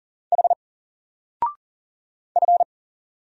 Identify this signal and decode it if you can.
Morse code, heard as a keyed tone, about 40 wpm, 700 Hz. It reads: H F
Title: Single Letter-Number 40wpm